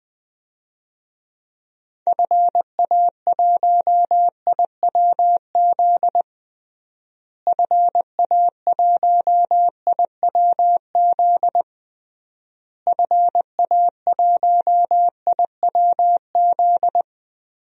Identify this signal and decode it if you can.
Morse code, heard as a keyed tone, about 20 wpm, 700 Hz. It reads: FA1IWZ FA1IWZ FA1IWZ
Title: DXCC-100 Call Signs - 6 Characters - Repeated 3x - 20wpm